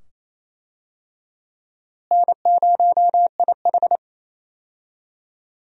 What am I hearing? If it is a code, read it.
Morse code, heard as a keyed tone, about 27 wpm, 700 Hz. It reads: N0IH
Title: USA Call Signs - 4 Characters - Repeated 2x - 28wpm